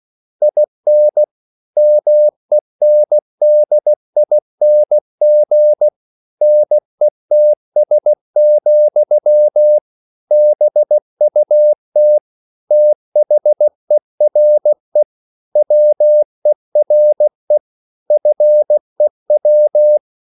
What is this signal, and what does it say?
Morse code, heard as a keyed tone, about 16 wpm, 600 Hz. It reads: IN MENDING NETS, BUT THERE WERE FEW